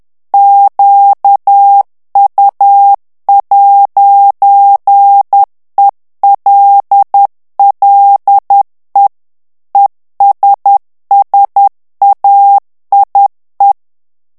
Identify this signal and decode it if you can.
Morse code, heard as a keyed tone, about 10 wpm, 800 Hz. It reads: QU'ELLE ESSAIE
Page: Français - texte en clair 1 - vitesse : 10 mots minute